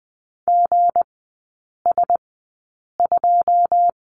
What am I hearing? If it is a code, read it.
Morse code, heard as a keyed tone, about 20 wpm, spaced as for 8 wpm, 700 Hz. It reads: GS2
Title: 178 Lesson 29e - Slash - Three Characters 20wpm